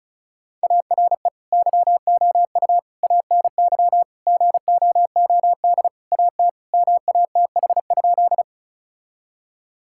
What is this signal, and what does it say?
Morse code, heard as a keyed tone, about 36 wpm, 700 Hz. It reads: ARE YOU ANY GOOD AT MATH?
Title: Sentences from Top 1000 Words - Review Part 6 - 35wpm